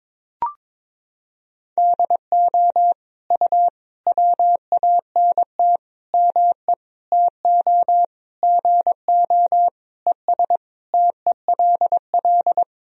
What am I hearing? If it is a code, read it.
Morse code, heard as a keyed tone, about 22 wpm, 700 Hz. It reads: DO U WANT ME TO GO ES TELL
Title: Sentences from Top 200 Words and Abbreviations 22wpm